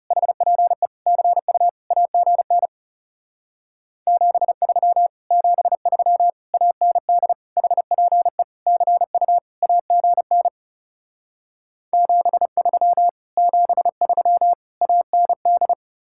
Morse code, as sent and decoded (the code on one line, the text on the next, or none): .... .--. / -.-. ..- / .- --. -. / --... ...-- / --... ...-- / .- -. -.. / .... .--. / -.-. ..- / .- --. -. / --... ...-- / --... ...-- / .- -. -..
HPE CU AGN 73 73 AND HPE CU AGN 73 73 AND